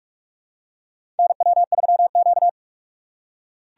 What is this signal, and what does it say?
Morse code, heard as a keyed tone, about 43 wpm, 700 Hz. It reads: NW3Q
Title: USA Call Signs - 4 Characters - 6x Character Spacing - 45wpm